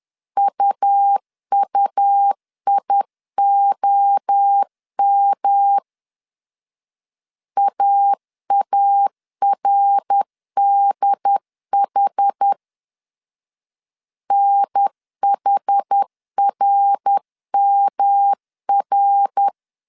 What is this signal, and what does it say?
Morse code, heard as a keyed tone, about 10 wpm, 800 Hz. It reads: UUIOM AARDH NHRMR
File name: CW / DET / 1 3 1500C S A 10wpm 3447